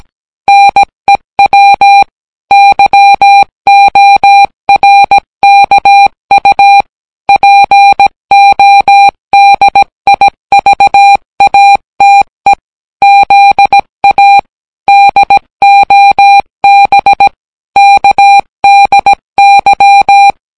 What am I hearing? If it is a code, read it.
Morse code, heard as a keyed tone, about 17 wpm, 800 Hz. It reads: NEW YORKU PODIVATE ZA DOB KDY